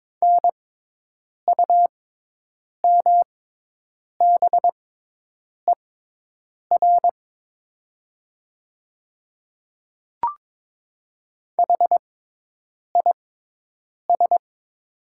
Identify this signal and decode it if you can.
Morse code, heard as a keyed tone, about 22 wpm, spaced as for 8 wpm, 700 Hz. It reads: NUMBER HIS
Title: 6x Character Spacing - Top 100 Words - 22wpm